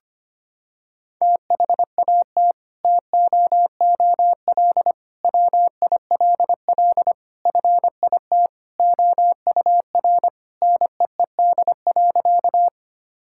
Morse code, heard as a keyed tone, about 25 wpm, 700 Hz. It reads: THAT TOOL WILL FIT OUR NEED.
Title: Sentences from Top 1000 Words - Review Part 6 - 25wpm